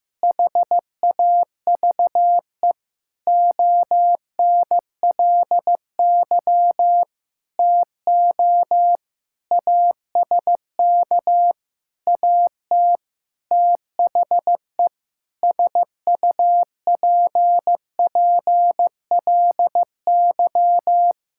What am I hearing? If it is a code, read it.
Morse code, heard as a keyed tone, about 15 wpm, 700 Hz. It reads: HAVE ONLY TO ASK AT THE SUPPLY
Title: Sentences from Top 1000 Words - Review Part 5 - 15wpm